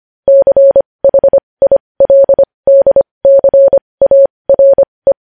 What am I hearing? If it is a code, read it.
Morse code, heard as a keyed tone, about 25 wpm, spaced as for 20 wpm, 550 Hz. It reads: CHILDCARE
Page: news headlines in morse code at 20 wpm